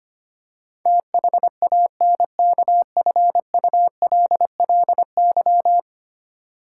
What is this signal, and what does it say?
Morse code, heard as a keyed tone, about 25 wpm, 700 Hz. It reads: THANKFULLY
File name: 10 Letter Words - First 3 Letters Given - 25wpm